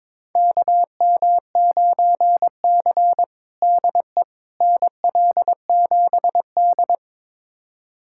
Morse code, heard as a keyed tone, about 22 wpm, 700 Hz. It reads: KM9C DE NL7D
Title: USA Callsign DE Callsign - 4 Characters - 22wpm